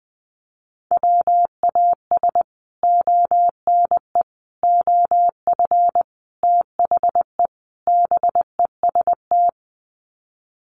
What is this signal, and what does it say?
Morse code, heard as a keyed tone, about 20 wpm, 700 Hz. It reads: WAS ONE OF THE BEST
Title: Sets of 5 Words Encore2 20wpm